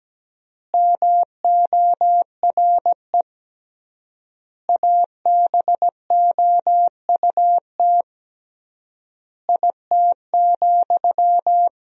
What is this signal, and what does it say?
Morse code, heard as a keyed tone, about 17 wpm, 700 Hz. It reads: MORE ABOUT IT,
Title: 3x Word-Spacing - Sentences from Top 200 Words - 17wpm